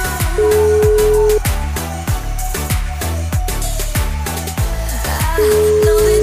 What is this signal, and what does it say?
Ringback tone (Europe, 425 Hz), with noise.